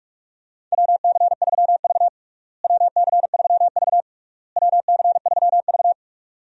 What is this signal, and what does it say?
Morse code, heard as a keyed tone, about 47 wpm, 700 Hz. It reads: WC3V WC3V WC3V
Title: USA Call Signs - 4 Characters - Repeated 3x - 45wpm